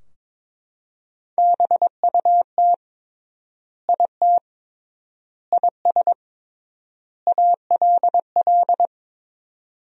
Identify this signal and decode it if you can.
Morse code, heard as a keyed tone, about 22 wpm, 700 Hz. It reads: BUT IT IS ALL
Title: 3x Word-Spacing - Sentences from Top 300 Words - 22wpm